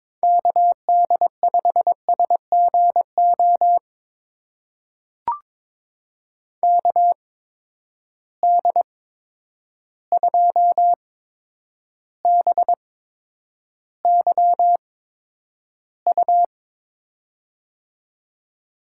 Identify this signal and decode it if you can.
Morse code, heard as a keyed tone, about 22 wpm, spaced as for 9 wpm, 700 Hz. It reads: KD5SGO KD2BYU
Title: USA Call Signs - 6 Characters - 8x Character Spacing - 22wpm